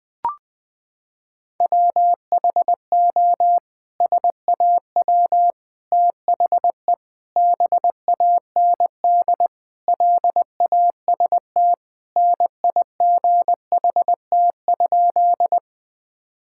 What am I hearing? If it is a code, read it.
Morse code, heard as a keyed tone, about 20 wpm, 700 Hz. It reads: WHO SAW THE BAND LAST NIGHT?
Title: Sentences from Top 1600 Words 20wpm